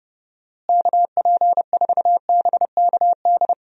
Morse code, heard as a keyed tone, about 32 wpm, 700 Hz. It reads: KP4BKD
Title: DXCC-100 Call Signs - 6 Characters - Repeated 3x - 30wpm